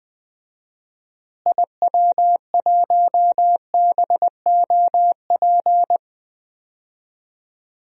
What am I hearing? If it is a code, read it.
Morse code, heard as a keyed tone, about 21 wpm, 700 Hz. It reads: IW1BOP
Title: DXCC-100 Call Signs - 6 Characters - Repeated 2x - 20wpm